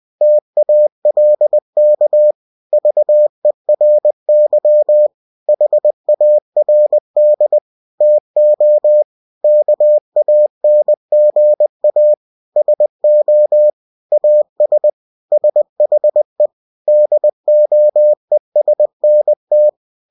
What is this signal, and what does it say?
Morse code, heard as a keyed tone, about 20 wpm, 600 Hz. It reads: TALK VERY HARD TO KANGA SO AS SHE DOESNT
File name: cw-books / winnie-the-pooh / winnie-20wpm-page-0065